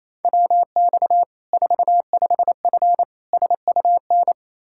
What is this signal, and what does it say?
Morse code, heard as a keyed tone, about 28 wpm, 700 Hz. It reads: WX 45F SUN
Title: Practice QSO - Exchange 2 - Info - 28wpm